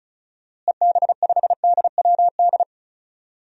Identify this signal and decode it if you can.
Morse code, heard as a keyed tone, about 34 wpm, 700 Hz. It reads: EB5DWD
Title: DXCC-100 Call Signs - 6 Characters - Repeated 3x - 35wpm